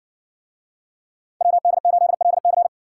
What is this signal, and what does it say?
Morse code, heard as a keyed tone, about 59 wpm, 700 Hz. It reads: WD8LC